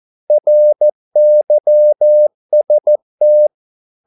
Morse code, heard as a keyed tone, about 14 wpm, 600 Hz. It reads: RYST
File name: Korte ord DK 14wpm